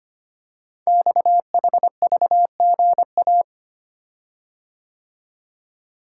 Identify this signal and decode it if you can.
Morse code, heard as a keyed tone, about 25 wpm, 700 Hz. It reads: XHVGA
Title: Code Groups - 5 Letters - 25wpm